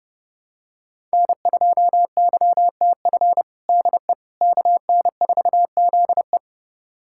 Morse code, heard as a keyed tone, about 30 wpm, 700 Hz. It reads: N2YTF DE KN4ZE
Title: USA Callsign DE Callsign - 5 Characters - 30wpm